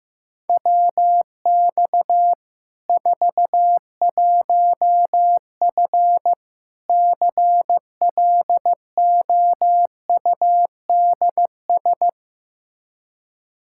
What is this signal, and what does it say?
Morse code, heard as a keyed tone, about 15 wpm, 700 Hz. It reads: WX 41F CLOUDS
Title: Practice QSO - Exchange 2 - Info - 15wpm